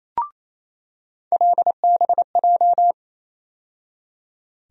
Morse code, heard as a keyed tone, about 28 wpm, 700 Hz. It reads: LBJ
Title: USA Call Sign Postfixes - 3 Letters 28wpm